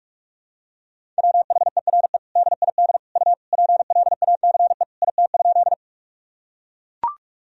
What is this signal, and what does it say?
Morse code, heard as a keyed tone, about 43 wpm, 700 Hz. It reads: WHERE DID U PLACE IT?
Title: Sentences from Top 200 Words and Abbreviations 45wpm